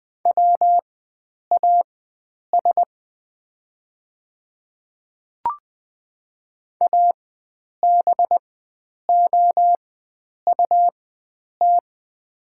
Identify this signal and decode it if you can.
Morse code, heard as a keyed tone, about 20 wpm, spaced as for 9 wpm, 700 Hz. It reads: WAS ABOUT
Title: 4x Character Spacing - Top 100 Words - 20wpm